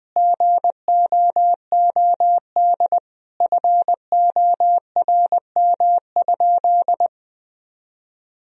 Morse code, heard as a keyed tone, about 21 wpm, 700 Hz. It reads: GOOD FORM?
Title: Sentences from Top 1000 Words - Review Part 1 - 20wpm